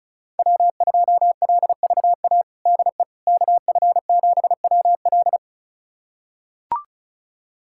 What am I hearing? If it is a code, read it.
Morse code, heard as a keyed tone, about 34 wpm, 700 Hz. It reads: W2LVA DE KF7WL
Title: USA Callsign DE Callsign - 5 Characters - 35wpm